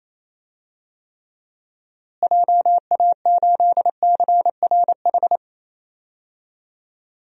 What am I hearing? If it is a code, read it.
Morse code, heard as a keyed tone, about 28 wpm, 700 Hz. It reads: JA8CRH
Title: DXCC-100 Call Signs - 6 Characters - Repeated 3x - 28wpm